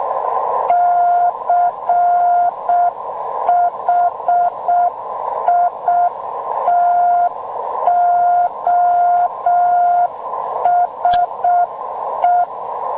Morse code, heard as a keyed tone, about 6 wpm, 700 Hz, with noise. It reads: CHITOSE